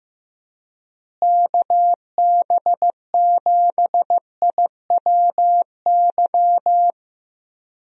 Morse code, heard as a keyed tone, about 15 wpm, 700 Hz. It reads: KB7IWY
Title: USA Call Signs - 6 Characters - Repeated 3x - 15wpm